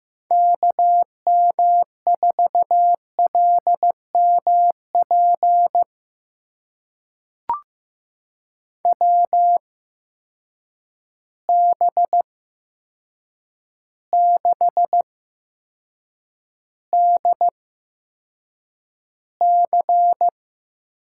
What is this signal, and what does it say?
Morse code, heard as a keyed tone, about 15 wpm, spaced as for 7 wpm, 700 Hz. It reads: KM4LMP WB6DC
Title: USA Call Signs - 6 Characters - 8x Character Spacing - 15wpm